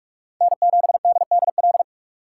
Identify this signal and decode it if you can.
Morse code, heard as a keyed tone, about 44 wpm, 700 Hz. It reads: N7DDL